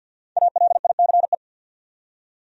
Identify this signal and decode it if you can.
Morse code, heard as a keyed tone, about 51 wpm, 700 Hz. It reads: ALICE
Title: USA Common Names - Max 5 Letters - 50wpm